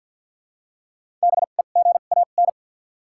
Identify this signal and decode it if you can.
Morse code, heard as a keyed tone, about 53 wpm, 700 Hz. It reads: B E G A N
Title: 2x Character Spacing - Top 300 Words - 50wpm